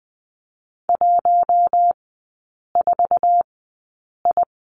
Morse code, heard as a keyed tone, about 20 wpm, spaced as for 8 wpm, 700 Hz. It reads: 14I
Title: 100 Lesson 17e - Letter u - Three Characters 20wpm